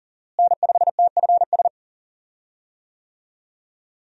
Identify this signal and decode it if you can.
Morse code, heard as a keyed tone, about 40 wpm, 700 Hz. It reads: N5TFS
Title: USA Call Signs - 5 Characters - Repeated 2x - 40wpm